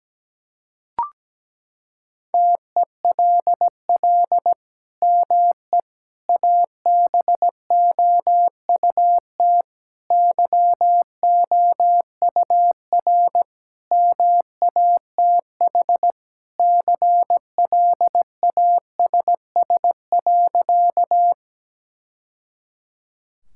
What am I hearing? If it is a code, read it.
Morse code, heard as a keyed tone, about 17 wpm, 700 Hz. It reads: TELL ME ABOUT YOUR MATH CLASS.
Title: Sentences from Top 1000 Words - Review Part 6 - 17wpm